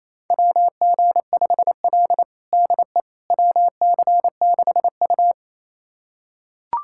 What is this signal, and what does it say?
Morse code, heard as a keyed tone, about 27 wpm, 700 Hz. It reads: WG5L DE WC6U
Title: USA Callsign DE Callsign - 4 Characters - 28wpm